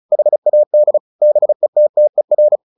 Morse code, heard as a keyed tone, about 34 wpm, 600 Hz. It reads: HAD BETTER
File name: the old man and the sea / the old man and the sea 1 35wpm